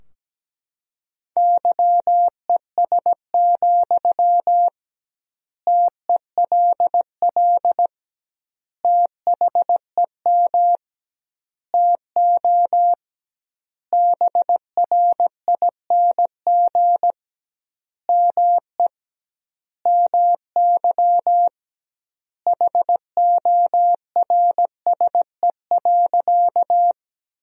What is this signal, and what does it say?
Morse code, heard as a keyed tone, about 17 wpm, 700 Hz. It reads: YES, TELL THEM TO BRING ME MY HORSE.
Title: 2x Word-Spacing - Sentences from Top 500 Words - 17wpm